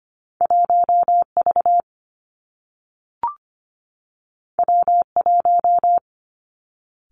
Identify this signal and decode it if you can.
Morse code, heard as a keyed tone, about 25 wpm, 700 Hz. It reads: 1 V W 1
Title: Two Challenging Letter-Number Combinations 25wpm